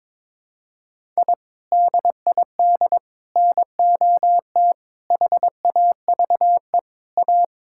Morse code, heard as a keyed tone, about 22 wpm, 700 Hz. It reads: I DID NOT HAVE A
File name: practice / 22wpm / Build A Phrase - 5-Word Phrases - 22wpm